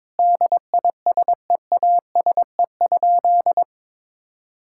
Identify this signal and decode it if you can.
Morse code, heard as a keyed tone, about 22 wpm, 700 Hz. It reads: DISEASE?
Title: Sentences from Top 1000 Words - Review Part 6 - 22wpm